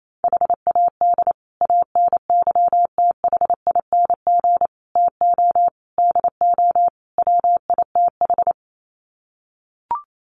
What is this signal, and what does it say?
Morse code, heard as a keyed tone, about 28 wpm, 690 Hz. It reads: HAD ANYTHING TO DO WITH
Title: Sets of 5 Words Encore 28wpm